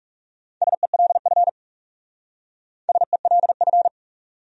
Morse code, heard as a keyed tone, about 36 wpm, 700 Hz. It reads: SELF SELF
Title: Speed-Racing Top 200 Words 30wpm